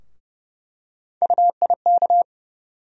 Morse code, heard as a keyed tone, about 30 wpm, 700 Hz. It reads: UIK
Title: USA Call Sign Postfixes - 3 Letters 30wpm